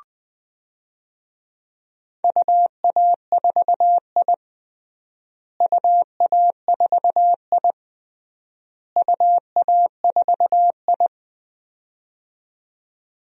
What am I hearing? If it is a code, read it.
Morse code, heard as a keyed tone, about 20 wpm, 700 Hz. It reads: UA4I UA4I UA4I
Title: DXCC-100 Call Signs - 4 Characters - Repeated 3x - 20wpm